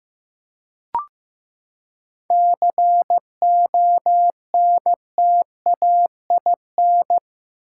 Morse code, heard as a keyed tone, about 15 wpm, 700 Hz. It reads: CONTAIN
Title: Seven Letter Words - 15wpm